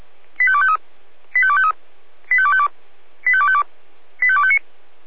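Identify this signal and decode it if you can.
Five-tone selective call (ZVEI), heard as quick runs of five tones, tones of about 70 ms. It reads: 85253 85253 85252 85252 85259
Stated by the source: ZVEI five-tone selective call received with an SDR.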